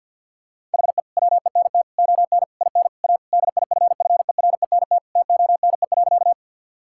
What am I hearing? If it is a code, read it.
Morse code, heard as a keyed tone, about 50 wpm, 700 Hz. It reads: HE WENT ON IN A DIFFERENT TONE.
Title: Sentences from Top 1600 Words 50wpm